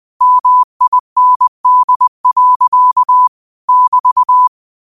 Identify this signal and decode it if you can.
Morse code, heard as a keyed tone, about 20 wpm, 1000 Hz. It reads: MIND. <BT>
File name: qotdfiles / 20wpm